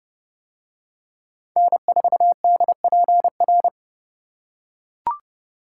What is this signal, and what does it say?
Morse code, heard as a keyed tone, about 29 wpm, 700 Hz. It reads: N4DPR